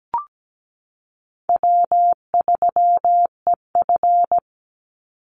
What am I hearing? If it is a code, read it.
Morse code, heard as a keyed tone, about 17 wpm, 700 Hz. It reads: W3EF
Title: Easy Call Signs Repeated 2x - 17wpm